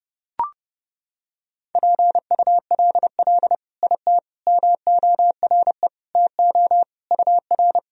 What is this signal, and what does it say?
Morse code, heard as a keyed tone, about 30 wpm, 700 Hz. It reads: PULL IT MORE TO UR
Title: Sentences from Top 400 Words and Abbreviations 30wpm